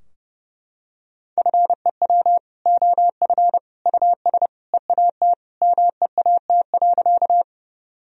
Morse code, heard as a keyed tone, about 30 wpm, 700 Hz. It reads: FEW OF US EAT MEAT.
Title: Sentences from Top 1000 Words - Review Part 3 - 30wpm